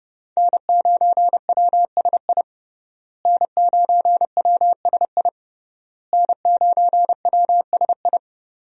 Morse code, heard as a keyed tone, about 28 wpm, 700 Hz. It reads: N9WSI N9WSI N9WSI
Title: USA Call Signs - 5 Characters - Repeated 3x - 30wpm